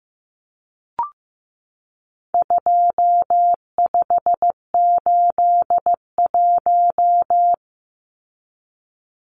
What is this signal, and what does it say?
Morse code, heard as a keyed tone, about 15 wpm, 700 Hz. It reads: 2581
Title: Numbers - 4-Digit 15wpm